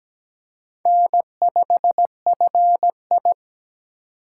Morse code, heard as a keyed tone, about 17 wpm, 700 Hz. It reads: N5FI
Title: USA Call Signs - 4 Characters - 8x Character Spacing - 17wpm